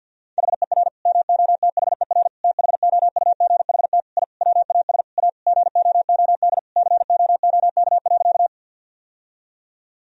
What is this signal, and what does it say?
Morse code, heard as a keyed tone, about 50 wpm, 700 Hz. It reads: HER MOTHER THOUGHT I WAS A GOOD COOK.
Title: Sentences from Top 1000 Words - Review Part 4 - 50wpm